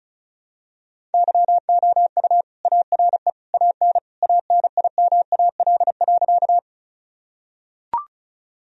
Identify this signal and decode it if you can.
Morse code, heard as a keyed tone, about 36 wpm, 700 Hz. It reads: YOU ARE AN ANIMAL.
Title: Sentences from Top 1000 Words - Review Part 1 - 35wpm